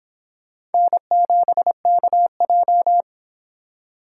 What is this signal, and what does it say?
Morse code, heard as a keyed tone, about 29 wpm, 700 Hz. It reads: N7KJ